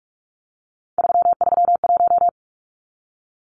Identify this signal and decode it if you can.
Morse code, heard as a keyed tone, about 44 wpm, 700 Hz. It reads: 331
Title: Numbers - 3-Digit 45wpm